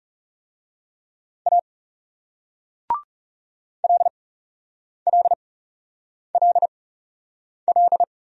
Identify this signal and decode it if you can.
Morse code, heard as a keyed tone, about 40 wpm, spaced as for 8 wpm, 700 Hz. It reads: A LLLL